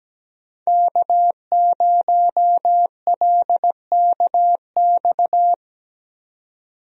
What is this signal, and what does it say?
Morse code, heard as a keyed tone, about 17 wpm, 700 Hz. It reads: K0LKX
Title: USA Call Signs - 5 Characters - 17wpm